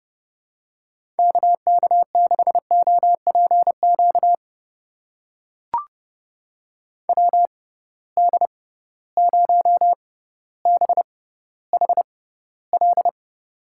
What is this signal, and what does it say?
Morse code, heard as a keyed tone, about 30 wpm, spaced as for 15 wpm, 700 Hz. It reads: KK6OPQ WD0BHL
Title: USA Call Signs - 6x Character Spacing - 30wpm